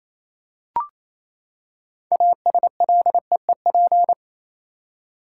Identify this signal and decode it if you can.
Morse code, heard as a keyed tone, about 28 wpm, 700 Hz. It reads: ASLEEP
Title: Six Letter Words - 28wpm